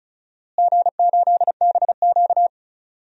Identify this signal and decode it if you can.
Morse code, heard as a keyed tone, about 34 wpm, 700 Hz. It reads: G8BQ
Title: DXCC-100 Call Signs - 4 Characters - Repeated 2x - 35wpm